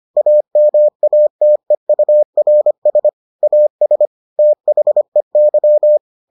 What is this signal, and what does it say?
Morse code, heard as a keyed tone, about 25 wpm, 600 Hz. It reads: AMATEURS AS THEY